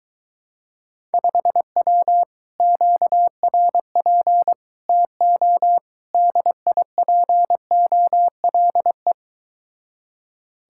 Morse code, heard as a keyed tone, about 23 wpm, 700 Hz. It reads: 5W QRP TO DIPOLE